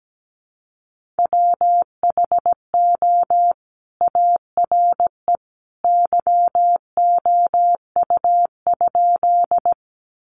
Morse code, heard as a keyed tone, about 17 wpm, 700 Hz. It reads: WHO ARE YOU?
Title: Sentences from Top 100 Words 17wpm